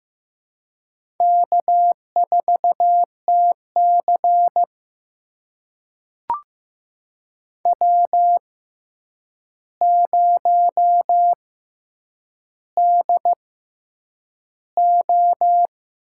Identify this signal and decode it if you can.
Morse code, heard as a keyed tone, about 15 wpm, spaced as for 8 wpm, 700 Hz. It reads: K4TC W0DO
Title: USA Call Signs - 4 Characters - 6x Character Spacing - 15wpm